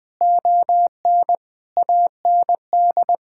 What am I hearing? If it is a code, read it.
Morse code, heard as a keyed tone, about 20 wpm, 700 Hz. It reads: ON AND
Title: Sentences with 7-Word Phrases - 20wpm